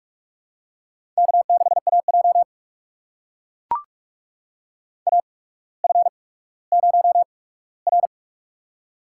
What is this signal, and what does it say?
Morse code, heard as a keyed tone, about 44 wpm, spaced as for 19 wpm, 700 Hz. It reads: K6AJ AF0R
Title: USA Call Signs - 4 Characters - 8x Character Spacing - 45wpm